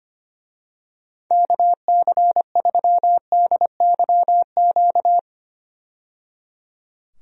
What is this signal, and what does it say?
Morse code, heard as a keyed tone, about 25 wpm, 700 Hz. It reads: KC3DYQ
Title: USA Call Signs - 6 Characters - 25wpm